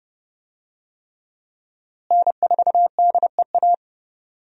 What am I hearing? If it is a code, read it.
Morse code, heard as a keyed tone, about 30 wpm, 700 Hz. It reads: N4DEA
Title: USA Call Signs - 5 Characters - Repeated 3x - 30wpm